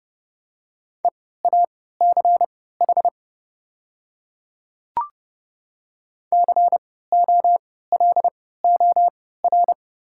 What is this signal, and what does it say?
Morse code, heard as a keyed tone, about 30 wpm, spaced as for 17 wpm, 700 Hz. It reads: EACH COLOR
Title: 3x Character Spacing - Top 300 Words - 30wpm